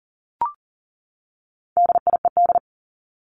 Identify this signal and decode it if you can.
Morse code, heard as a keyed tone, about 40 wpm, 700 Hz. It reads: DIED
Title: ED Suffix - Top 100 Words with Base Words 40wpm